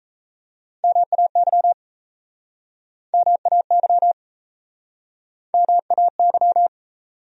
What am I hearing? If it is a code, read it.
Morse code, heard as a keyed tone, about 39 wpm, 700 Hz. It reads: MAY MAY MAY